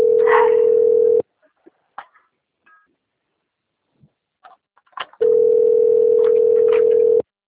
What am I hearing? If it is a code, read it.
Ringback tone (North America), 440 + 480 Hz.